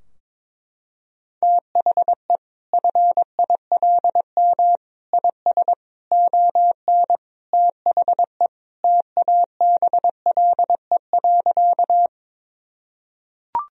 Morse code, heard as a keyed tone, about 22 wpm, 700 Hz. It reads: THE FILM IS ON THE TABLE.
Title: Sentences from Top 1000 Words - Review Part 3 - 22wpm